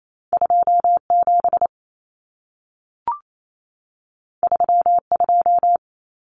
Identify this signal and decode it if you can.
Morse code, heard as a keyed tone, about 28 wpm, 700 Hz. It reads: 27 32
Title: Numbers - 2-Digit 28wpm